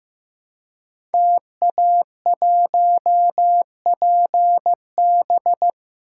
Morse code, heard as a keyed tone, about 15 wpm, 700 Hz. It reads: TA1PB
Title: Code Groups - 5 Letter:Numbers - 15wpm